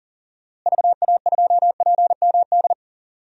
Morse code, heard as a keyed tone, about 38 wpm, 700 Hz. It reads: VA2PMD
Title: DXCC-100 Call Signs - 6 Characters - Repeated 3x - 40wpm